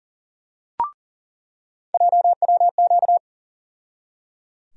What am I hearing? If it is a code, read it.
Morse code, heard as a keyed tone, about 38 wpm, 700 Hz. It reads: JWQ
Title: USA Call Sign Postfixes - 3 Letters 40wpm